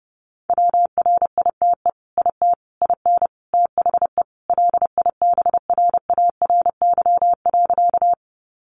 Morse code, heard as a keyed tone, about 28 wpm, 700 Hz. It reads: WRITE IT IN THE LIBRARY.
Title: Sentences from Top 600 Words 30wpm